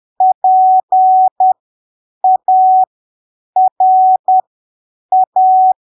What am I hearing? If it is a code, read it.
Morse code, heard as a keyed tone, about 10 wpm, spaced as for 7 wpm, 750 Hz. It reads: PARA
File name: arquivos / texto / texto 21 10ppm